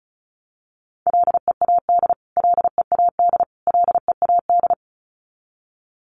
Morse code, heard as a keyed tone, about 34 wpm, 700 Hz. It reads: LEAD LEAD LEAD
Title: Top 500 Words Repeated 35wpm